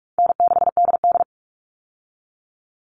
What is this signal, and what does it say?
Morse code, heard as a keyed tone, about 44 wpm, 700 Hz. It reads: N6DD
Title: All Cut Numbers - 4-Digit 45wpm